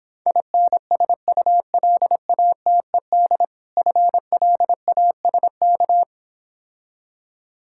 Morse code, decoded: INSULATED FLASK

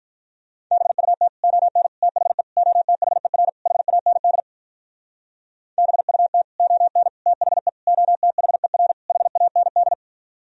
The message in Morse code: BUT ON THE OTHER HAND BUT ON THE OTHER HAND

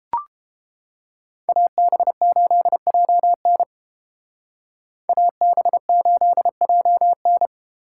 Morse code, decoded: AB8JN AB8JN

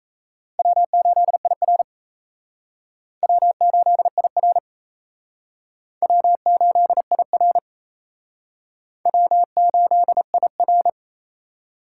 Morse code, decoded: W8IR W8IR W8IR W8IR

